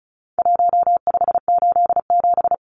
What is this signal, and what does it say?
Morse code, heard as a keyed tone, about 34 wpm, 700 Hz. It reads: 1587
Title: Numbers - Contest Style - 4-Digit 35wpm